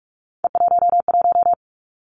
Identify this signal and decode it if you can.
Morse code, heard as a keyed tone, about 46 wpm, 700 Hz. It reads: E11
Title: Three Challenging Letter-Number Combinations 45wpm